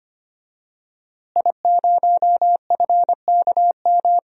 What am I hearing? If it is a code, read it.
Morse code, heard as a keyed tone, about 25 wpm, 700 Hz. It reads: I0FKM
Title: DXCC-100 Call Signs - 5 Characters - Repeated 3x - 25wpm